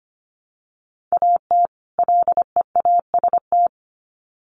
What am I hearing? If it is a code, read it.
Morse code, heard as a keyed tone, about 25 wpm, 700 Hz. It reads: AT LEAST
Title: Sets of 2 Words 25wpm